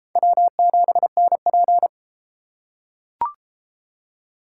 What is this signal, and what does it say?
Morse code, heard as a keyed tone, about 32 wpm, 700 Hz. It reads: W7NP